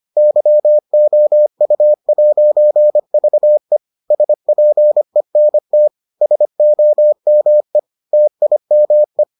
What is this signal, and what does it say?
Morse code, heard as a keyed tone, about 25 wpm, 600 Hz. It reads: YOU'VE SPENT SOME TIME